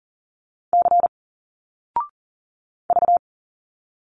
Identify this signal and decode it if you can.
Morse code, heard as a keyed tone, about 40 wpm, 700 Hz. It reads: C V
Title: Single Letters 40wpm